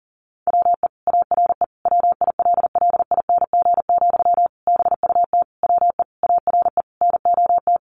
Morse code, heard as a keyed tone, about 40 wpm, 700 Hz. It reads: WE ARE WILLING, BUT WE ARE NOT
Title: Sentences from Top 1300 Words 40wpm